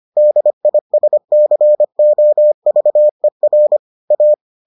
Morse code, heard as a keyed tone, about 25 wpm, 600 Hz. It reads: DISCOVER A